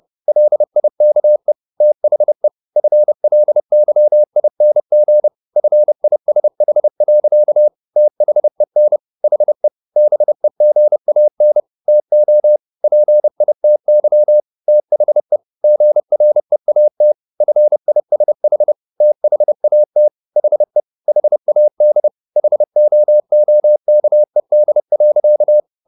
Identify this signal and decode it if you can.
Morse code, heard as a keyed tone, about 30 wpm, 600 Hz. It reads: LIKE THE FLYING FISH. THEN HE BEGAN TO PITY THE GREAT FISH THAT HE HAD HOOKED.
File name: the old man and the sea / the old man and the sea 1 30wpm